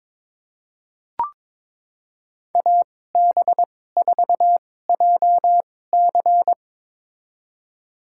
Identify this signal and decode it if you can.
Morse code, heard as a keyed tone, about 23 wpm, spaced as for 16 wpm, 690 Hz. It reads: AB4JC
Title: USA Call Signs - 5 Characters - 2x Character Spacing - 22wpm